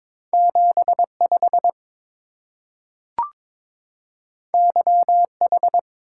Morse code, heard as a keyed tone, about 22 wpm, 700 Hz. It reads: 75 YH
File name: practice / 22wpm / Two Letter-Number Combinations - 22wpm